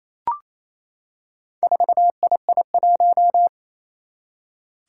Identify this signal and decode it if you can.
Morse code, heard as a keyed tone, about 28 wpm, 700 Hz. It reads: 4II1
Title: Four Challenging Letter-Number Combinations 28wpm